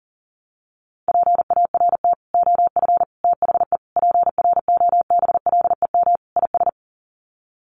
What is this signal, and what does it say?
Morse code, heard as a keyed tone, about 40 wpm, 700 Hz. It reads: PART OF THE PROBLEM IS
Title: Sets of 5 Words Encore2 40wpm